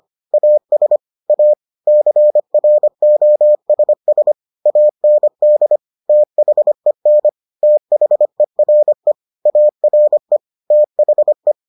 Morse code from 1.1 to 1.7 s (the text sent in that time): A